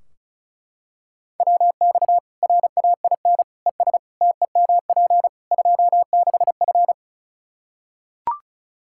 Morse code, heard as a keyed tone, about 34 wpm, 700 Hz. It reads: WX RAIN ES TEMP 26F
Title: Practice QSO - Exchange 2 - Info - 35wpm